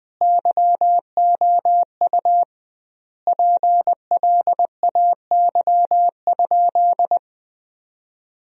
Morse code, heard as a keyed tone, about 20 wpm, 700 Hz. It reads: YOU PLAY?
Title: 2x Word-Spacing - Sentences from Top 200 Words - 20wpm